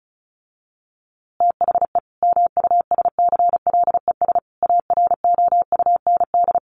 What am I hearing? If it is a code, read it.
Morse code, heard as a keyed tone, about 34 wpm, 700 Hz. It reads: THE MUSCLES AROUND
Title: A Sound Of Thunder 35wpm